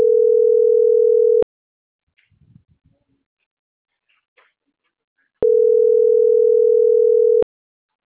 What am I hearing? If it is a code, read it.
Ringback tone (North America), 440 + 480 Hz.